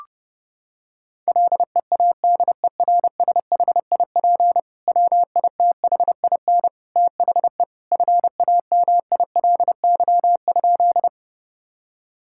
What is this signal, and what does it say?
Morse code, heard as a keyed tone, about 31 wpm, 700 Hz. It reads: LEADERSHIP WITHIN THE FAMILY?
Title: Sentences from Top 1600 Words 30wpm